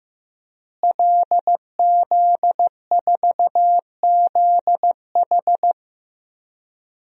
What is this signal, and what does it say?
Morse code, heard as a keyed tone, about 15 wpm, 700 Hz. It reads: LZ4ZH
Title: DXCC-100 Call Signs - 5 Characters - Repeated 3x - 15wpm